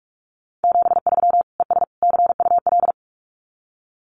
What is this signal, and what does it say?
Morse code, heard as a keyed tone, about 46 wpm, 700 Hz. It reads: 73 ES CUL
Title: Common QSO Phrases Encore 45wpm